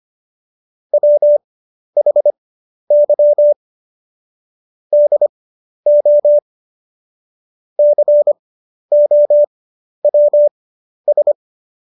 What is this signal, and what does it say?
Morse code, heard as a keyed tone, about 25 wpm, spaced as for 11 wpm, 590 Hz. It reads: WHY DO COWS
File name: Jokes-25-char-10-eff